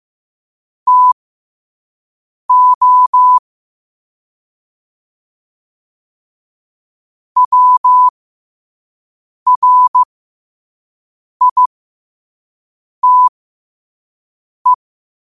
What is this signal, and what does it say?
Morse code, heard as a keyed tone, about 15 wpm, spaced as for 5 wpm, 1000 Hz. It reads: TO WRITE
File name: qotdfiles / 5wpm